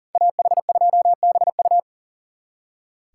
Morse code, heard as a keyed tone, about 52 wpm, 700 Hz. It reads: AH2BU